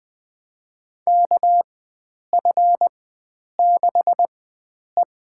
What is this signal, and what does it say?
Morse code, heard as a keyed tone, about 20 wpm, spaced as for 9 wpm, 700 Hz. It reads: KF6E
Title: USA Call Signs - 4x Character Spacing - 20wpm